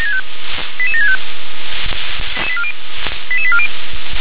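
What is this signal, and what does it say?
Five-tone selective call (ZVEI), heard as quick runs of five tones, tones of about 70 ms. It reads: 88765 88840
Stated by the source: Five-tone ZVEI selective call, received by SDR.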